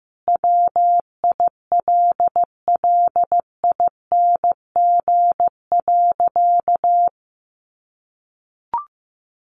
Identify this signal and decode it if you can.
Morse code, heard as a keyed tone, about 15 wpm, 700 Hz. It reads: WILLING.
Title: Sentences from Top 1400 Words 15wpm